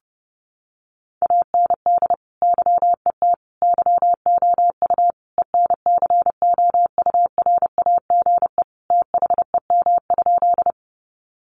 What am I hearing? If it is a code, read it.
Morse code, heard as a keyed tone, about 30 wpm, 700 Hz. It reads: AND YET YOU ENCOURAGE THEM?
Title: Sentences from Top 1400 Words 30wpm